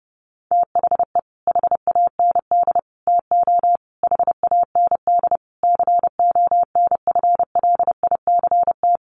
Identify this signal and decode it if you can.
Morse code, heard as a keyed tone, about 30 wpm, 700 Hz. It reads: THE HAND TO HAND CONFLICT
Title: Sentences from Top 1300 Words 30wpm